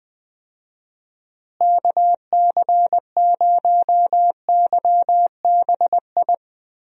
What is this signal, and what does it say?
Morse code, heard as a keyed tone, about 20 wpm, 700 Hz. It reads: KC0YBI
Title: USA Call Signs - 6 Characters - Repeated 2x - 20wpm